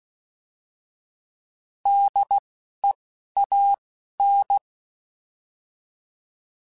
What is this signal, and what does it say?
Morse code, heard as a keyed tone, about 16 wpm, spaced as for 12 wpm, 800 Hz. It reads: DEAN
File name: Alphabetical / Alphabetical mp3s / Ds @10